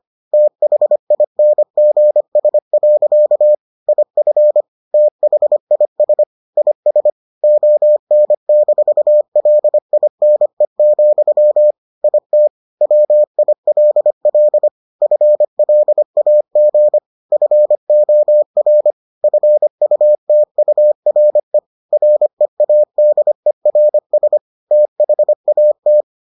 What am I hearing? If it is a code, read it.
Morse code, heard as a keyed tone, about 25 wpm, 600 Hz. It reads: THINGS. IF THIS IS ON-LINE, IT WILL FLAG FOR FUTURE READERS THAT